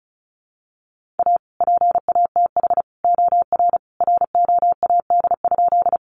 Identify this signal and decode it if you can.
Morse code, heard as a keyed tone, about 34 wpm, 700 Hz. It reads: A PATH OR ROAD?
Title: Sentences from Top 1400 Words 35wpm